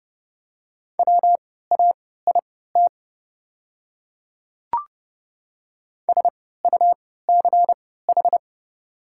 Morse code, heard as a keyed tone, about 31 wpm, spaced as for 17 wpm, 690 Hz. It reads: WAIT SUCH
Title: 3x Character Spacing - Top 500 Words - 30wpm